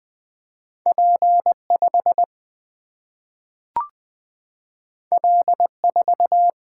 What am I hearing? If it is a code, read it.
Morse code, heard as a keyed tone, about 20 wpm, 700 Hz. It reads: P 5 L 4
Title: Single-Letter Single-Number 20wpm